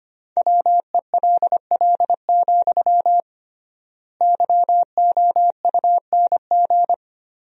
Morse code, heard as a keyed tone, about 25 wpm, 700 Hz. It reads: WELL, YOUNG
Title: 3x Word-Spacing - Sentences from Top 400 Words - 25wpm